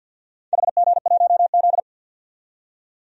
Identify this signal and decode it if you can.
Morse code, heard as a keyed tone, about 47 wpm, 700 Hz. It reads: HG1Z
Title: DXCC-100 Call Signs - 4 Characters - 50wpm